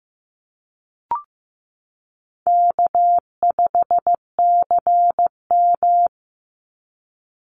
Morse code, heard as a keyed tone, about 14 wpm, 700 Hz. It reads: K5CM
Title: Call Signs Repeated 3x - 15wpm